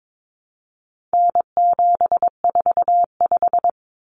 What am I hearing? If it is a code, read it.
Morse code, heard as a keyed tone, about 22 wpm, 700 Hz. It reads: N745
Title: Common Cut Numbers - 4-Digit 22wpm